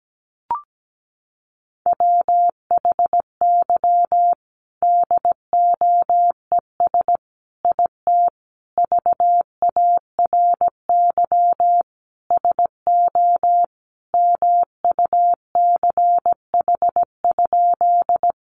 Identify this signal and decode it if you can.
Morse code, heard as a keyed tone, about 17 wpm, 700 Hz. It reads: WHY DOES IT VARY SO MUCH?
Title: Sentences from Top 1200 Words 17wpm